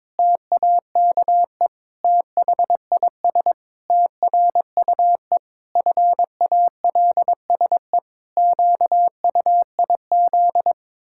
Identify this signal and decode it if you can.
Morse code, heard as a keyed tone, about 22 wpm, 700 Hz. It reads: TAKE THIS TRUE FALSE QUIZ
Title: Pangram Challenge v2 - 22wpm